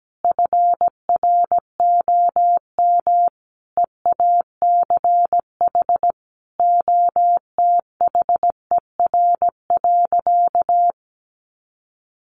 Morse code, heard as a keyed tone, about 17 wpm, 700 Hz. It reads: FROM EACH OTHER.
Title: Easy Phrases 17wpm